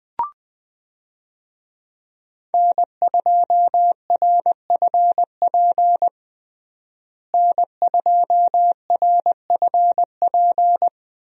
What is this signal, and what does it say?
Morse code, heard as a keyed tone, about 21 wpm, 700 Hz. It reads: N2RFP N2RFP
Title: USA Call Signs - 5 Characters - Repeated 3x - 20wpm